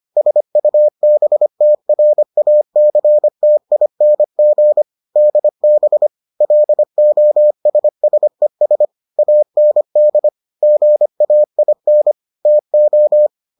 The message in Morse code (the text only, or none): SUBTRACTING DB LOSSES AND GAIN TO